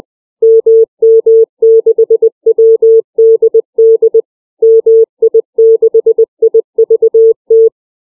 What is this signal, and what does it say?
Morse code, heard as a keyed tone, about 20 wpm, 450 Hz. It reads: MM6WDD MI6IVT
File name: CW / callsign